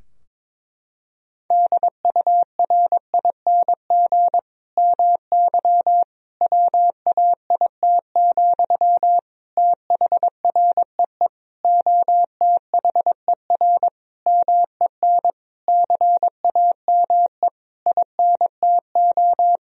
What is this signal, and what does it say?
Morse code, heard as a keyed tone, about 22 wpm, 700 Hz. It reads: DURING MY WAIT, THREE OTHER MEN CAME INTO
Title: Sentences from Top 1000 Words - Review Part 2 - 22wpm